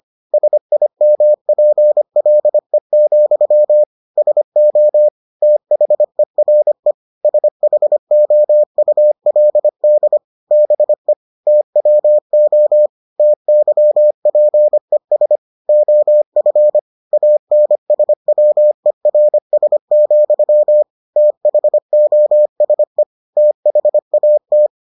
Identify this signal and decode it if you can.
Morse code, heard as a keyed tone, about 25 wpm, 600 Hz. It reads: SIMPLE, SO THERE SHOULD BE TWO TYPES OF ANSWERS, THOSE THAT